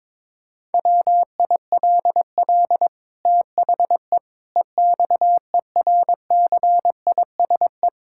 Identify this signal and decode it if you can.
Morse code, heard as a keyed tone, about 22 wpm, 700 Hz. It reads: WILL THE EXERCISE
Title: Sentences from Top 1000 Words - Review Part 6 - 22wpm